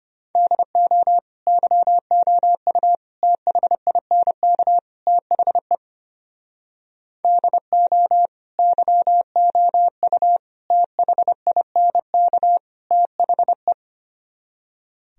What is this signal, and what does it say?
Morse code, heard as a keyed tone, about 31 wpm, 700 Hz. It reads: DO YOU THINK THE DO YOU THINK THE
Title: Speed-Racing - Sets of 4 Words 25wpm